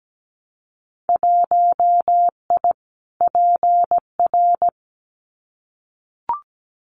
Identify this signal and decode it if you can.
Morse code, heard as a keyed tone, about 17 wpm, 700 Hz. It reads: 1I PR
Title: Winter Field Day Exchanges Encore 17wpm